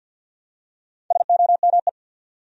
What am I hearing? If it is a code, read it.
Morse code, heard as a keyed tone, about 52 wpm, 700 Hz. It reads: SOME
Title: Top 100 Words 50wpm